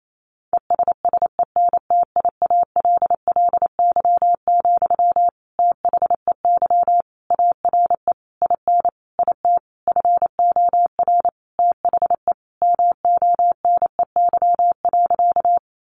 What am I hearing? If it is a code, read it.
Morse code, heard as a keyed tone, about 28 wpm, 700 Hz. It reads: ESSENTIALLY, THEY ARE IN IT FOR THE MONEY.
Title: Sentences from Top 1300 Words 28wpm